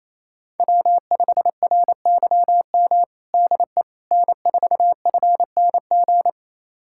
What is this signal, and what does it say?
Morse code, heard as a keyed tone, about 28 wpm, 700 Hz. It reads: W5RYM DE N4FNG
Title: USA Callsign DE Callsign - 5 Characters - 28wpm